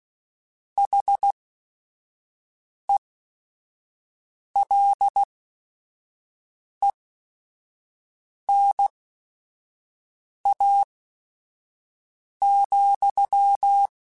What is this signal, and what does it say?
Morse code, heard as a keyed tone, about 16 wpm, spaced as for 5 wpm, 800 Hz. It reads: HELENA,
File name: QSOs / QSOs-mp3s / WD0BGP 1@5